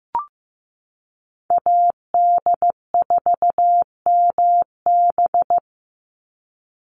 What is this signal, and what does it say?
Morse code, heard as a keyed tone, about 15 wpm, 700 Hz. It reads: AD4MB
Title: USA Call Signs 15wpm